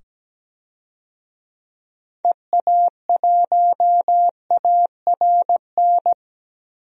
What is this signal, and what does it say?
Morse code, heard as a keyed tone, about 17 wpm, 700 Hz. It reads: EA1ARN
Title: DXCC-100 Call Signs - 6 Characters - 17wpm